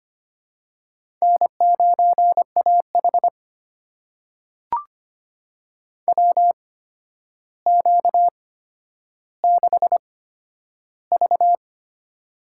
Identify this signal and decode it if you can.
Morse code, heard as a keyed tone, about 25 wpm, spaced as for 11 wpm, 700 Hz. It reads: N9AH WQ6V
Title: USA Call Signs - 4 Characters - 8x Character Spacing - 25wpm